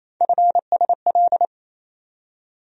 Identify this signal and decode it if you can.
Morse code, heard as a keyed tone, about 28 wpm, 700 Hz. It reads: FSL
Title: USA Call Sign Postfixes - 3 Letters 28wpm